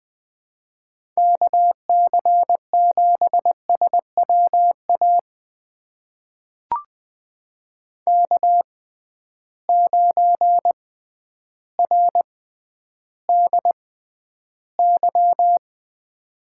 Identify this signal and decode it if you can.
Morse code, heard as a keyed tone, about 20 wpm, spaced as for 11 wpm, 700 Hz. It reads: KC7SWA K9RDY
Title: USA Call Signs - 6x Character Spacing - 20wpm